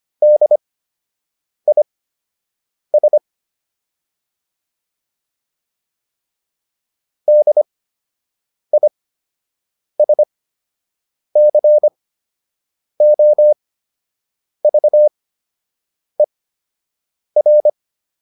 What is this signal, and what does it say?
Morse code, heard as a keyed tone, about 25 wpm, spaced as for 7 wpm, 600 Hz. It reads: DIS DISCOVER